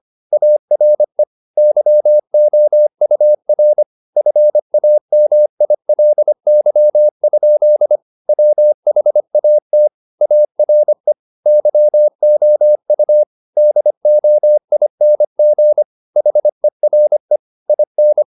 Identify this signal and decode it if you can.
Morse code, heard as a keyed tone, about 25 wpm, 600 Hz. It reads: ARE YOUR FAMILY? WHAT ARE YOU DOING HERE IN